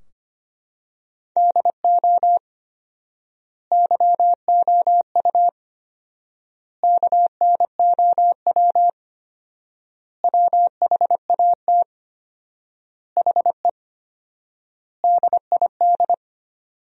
Morse code, heard as a keyed tone, about 25 wpm, 700 Hz. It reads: DO YOU KNOW WHAT HE DID
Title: 4x Word-Spacing - Sentences from Top 300 Words - 25wpm